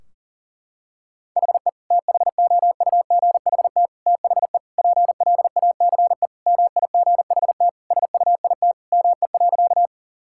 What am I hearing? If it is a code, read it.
Morse code, heard as a keyed tone, about 40 wpm, 700 Hz. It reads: HE THOUGHT THE PLACE MIGHT SUIT ME.
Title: Sentences from Top 1600 Words 40wpm